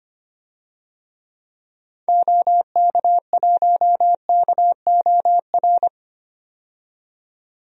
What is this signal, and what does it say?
Morse code, heard as a keyed tone, about 25 wpm, 700 Hz. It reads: OK1KOR
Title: DXCC-100 Call Signs - 6 Characters - 25wpm